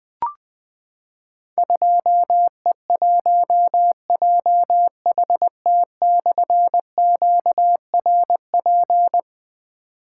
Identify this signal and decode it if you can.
Morse code, heard as a keyed tone, about 21 wpm, 700 Hz. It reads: 2E1JHT/QRP
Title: Difficult Call Signs 20wpm